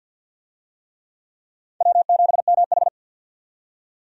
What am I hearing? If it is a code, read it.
Morse code, heard as a keyed tone, about 47 wpm, 700 Hz. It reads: W7MH
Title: USA Call Signs - 4 Characters - 50wpm